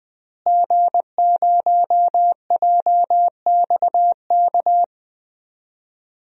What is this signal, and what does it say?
Morse code, heard as a keyed tone, about 20 wpm, 700 Hz. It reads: G0JXK
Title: DXCC-100 Call Signs - 5 Characters - Repeated 2x - 20wpm